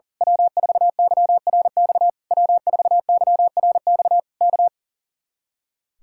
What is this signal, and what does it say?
Morse code, heard as a keyed tone, about 36 wpm, 700 Hz. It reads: W4YRX W4YRX K
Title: Speed-Racing Common QSO Phrases Encore - 40wpm